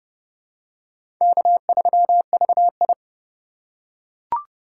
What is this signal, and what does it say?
Morse code, heard as a keyed tone, about 31 wpm, 700 Hz. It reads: K3VI
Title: USA Call Signs - 4 Characters - 6x Character Spacing - 30wpm